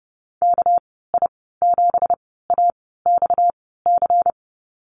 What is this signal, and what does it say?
Morse code, heard as a keyed tone, about 30 wpm, spaced as for 17 wpm, 700 Hz. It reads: KI7AXC
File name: Lesson 37g - Letter x - Call Signs 30wpm